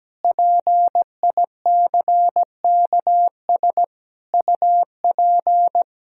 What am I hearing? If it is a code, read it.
Morse code, heard as a keyed tone, about 17 wpm, 700 Hz. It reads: PICKS UP
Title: Pangram Challenge v2 - 17wpm